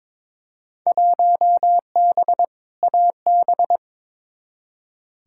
Morse code, heard as a keyed tone, about 22 wpm, 700 Hz. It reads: 1B AB
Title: Field Day Exchanges - 22wpm